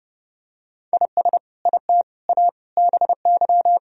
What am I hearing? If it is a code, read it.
Morse code, heard as a keyed tone, about 32 wpm, 700 Hz. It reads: IS IT A BY
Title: Sentences from Top 1000 Words - Review Part 2 - 30wpm